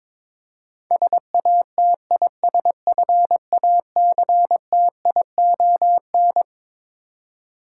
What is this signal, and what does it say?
Morse code, heard as a keyed tone, about 22 wpm, 700 Hz. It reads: SATISFACTION
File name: Twelve Letter Words - 22wpm